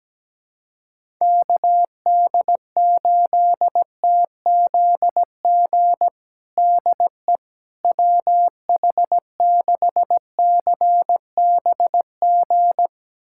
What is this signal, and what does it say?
Morse code, heard as a keyed tone, about 17 wpm, 700 Hz. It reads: KD8TZG DE WH6CBG
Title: USA Callsign DE Callsign - 6 Characters - 17wpm